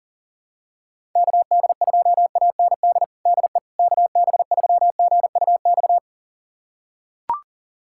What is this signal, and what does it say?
Morse code, heard as a keyed tone, about 40 wpm, 700 Hz. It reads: KD2AND DE KB3GUX
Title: USA Callsign DE Callsign - 6 Characters - 40wpm